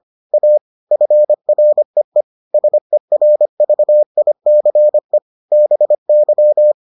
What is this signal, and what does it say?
Morse code, heard as a keyed tone, about 25 wpm, 600 Hz. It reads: A FREE SERVICE BY